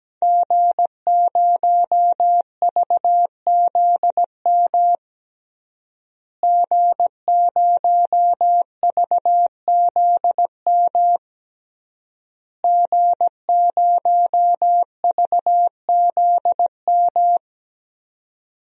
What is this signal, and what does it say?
Morse code, heard as a keyed tone, about 17 wpm, 700 Hz. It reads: G0VZM G0VZM G0VZM
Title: DXCC-100 Call Signs - 5 Characters - Repeated 3x - 17wpm